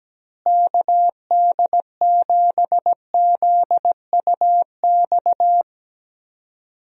Morse code, heard as a keyed tone, about 17 wpm, 700 Hz. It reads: KD7ZUX
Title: USA Call Signs - 6 Characters - Repeated 2x - 17wpm